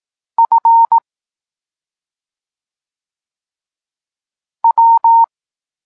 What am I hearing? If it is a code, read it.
Morse code, heard as a keyed tone, about 18 wpm, 920 Hz. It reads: F W
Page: Tekst 32 – Doskonalenie odbioru poznanych liter